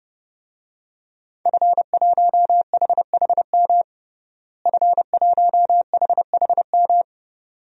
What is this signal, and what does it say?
Morse code, heard as a keyed tone, about 30 wpm, 700 Hz. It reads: F1HHM F1HHM
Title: DXCC-100 Call Signs - 5 Characters - Repeated 2x - 30wpm